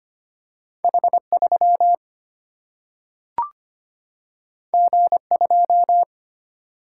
Morse code, heard as a keyed tone, about 25 wpm, 700 Hz. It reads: H 3 G 2
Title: Single-Letter Single-Number 25wpm